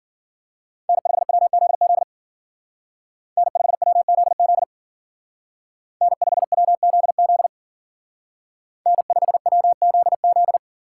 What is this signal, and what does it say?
Morse code, heard as a keyed tone, about 48 wpm, 700 Hz. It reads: N5WZZ N5WZZ N5WZZ N5WZZ
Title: Speed Racing - USA 1x3 Call Signs 40wpm